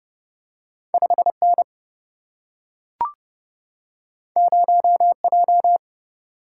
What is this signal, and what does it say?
Morse code, heard as a keyed tone, about 30 wpm, 700 Hz. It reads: 5N 0J
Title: Single-Letter Single-Number 30wpm